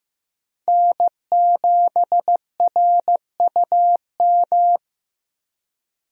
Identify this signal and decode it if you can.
Morse code, heard as a keyed tone, about 15 wpm, 700 Hz. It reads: N7RUM
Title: USA Call Signs - 5 Characters - Repeated 2x - 15wpm